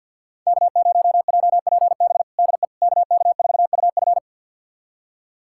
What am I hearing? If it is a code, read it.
Morse code, heard as a keyed tone, about 48 wpm, 700 Hz. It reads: K0JPB DE KK4UF
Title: USA Callsign DE Callsign - 5 Characters - 50wpm